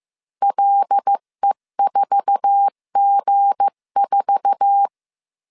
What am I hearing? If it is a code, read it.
Morse code, heard as a keyed tone, about 15 wpm, 800 Hz. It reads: LE4G4
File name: CW / DET / 1 9 1500C S B 14wpm 3302